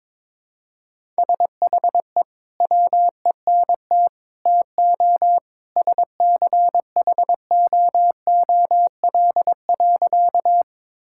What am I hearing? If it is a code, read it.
Morse code, heard as a keyed tone, about 22 wpm, 700 Hz. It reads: SHE WENT TO SCHOOL.
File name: practice / 22wpm / Sentences with 2-Word Phrases - 22wpm